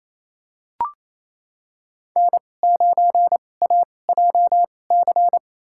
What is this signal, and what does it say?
Morse code, heard as a keyed tone, about 28 wpm, spaced as for 20 wpm, 700 Hz. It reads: N9AJC
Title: USA Call Signs - 5 Characters - 2x Character Spacing - 28wpm